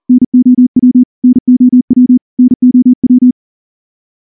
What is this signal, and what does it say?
Morse code, heard as a keyed tone, about 42 wpm, 250 Hz. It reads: NOW NOW NOW
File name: morse-training-40WPM